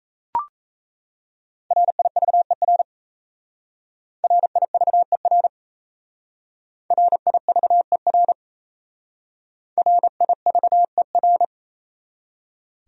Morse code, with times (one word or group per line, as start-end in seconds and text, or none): 1.7-2.8 RIVER
4.2-5.5 RIVER
6.9-8.3 RIVER
9.8-11.4 RIVER